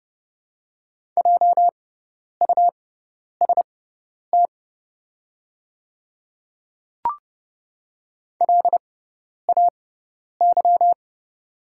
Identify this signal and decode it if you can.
Morse code, heard as a keyed tone, about 30 wpm, spaced as for 10 wpm, 690 Hz. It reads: JUST LAY
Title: 6x Character Spacing - Top 500 Words - 30wpm